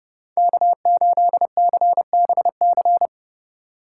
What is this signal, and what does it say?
Morse code, heard as a keyed tone, about 37 wpm, 700 Hz. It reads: K8CBC
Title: Speed Racing - USA Call Signs 30wpm